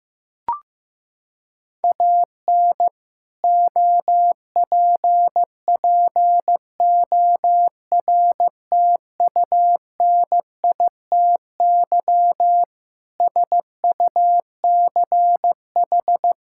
Morse code, heard as a keyed tone, about 15 wpm, 700 Hz. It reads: AN OPPORTUNITY SUCH
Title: Sentences from Top 1600 Words 15wpm